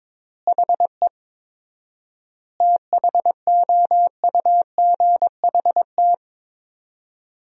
Morse code, heard as a keyed tone, about 22 wpm, 700 Hz. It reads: HE THOUGHT
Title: 4x Word-Spacing - Sentences from Top 300 Words - 22wpm